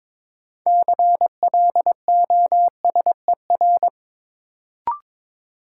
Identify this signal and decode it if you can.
Morse code, heard as a keyed tone, about 22 wpm, 700 Hz. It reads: CLOSER
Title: Sets of 5 Words Encore - New Word Drill - 22wpm